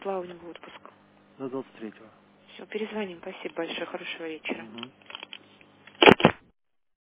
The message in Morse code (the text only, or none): none